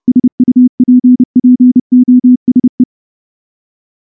Morse code, heard as a keyed tone, about 29 wpm, 250 Hz. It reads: SUPPOSE
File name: morse-training-30WPM